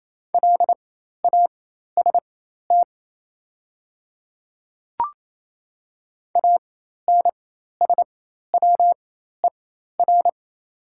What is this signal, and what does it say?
Morse code, heard as a keyed tone, about 27 wpm, spaced as for 13 wpm, 700 Hz. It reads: LAST ANSWER
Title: 4x Character Spacing - Top 300 Words - 28wpm